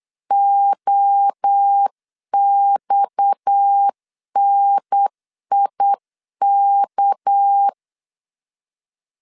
Morse code, heard as a keyed tone, about 8 wpm, 800 Hz. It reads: OXNIK
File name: CW / DET / 1 6 1500C S C 8wpm 5219